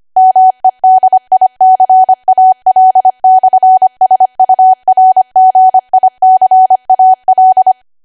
Morse code, heard as a keyed tone, about 25 wpm, 750 Hz. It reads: MEDICAL/SURGICAL
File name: Archive / 25 WPM / 25WPM